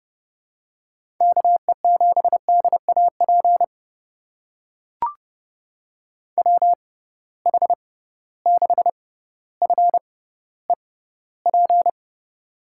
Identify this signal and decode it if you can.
Morse code, heard as a keyed tone, about 32 wpm, spaced as for 16 wpm, 700 Hz. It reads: KE7DAP WH6FEP
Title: USA Call Signs - 6 Characters - 6x Character Spacing - 30wpm